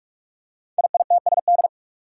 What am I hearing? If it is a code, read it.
Morse code, heard as a keyed tone, about 44 wpm, 700 Hz. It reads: IITSD
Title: Code Groups - 5 Letters - 45wpm